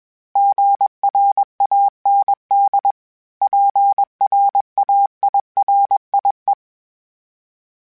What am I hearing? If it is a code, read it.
Morse code, heard as a keyed tone, about 21 wpm, 800 Hz. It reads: GRAND PRAIRIE
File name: fromSMs / FromSMs mp3s / Cities@20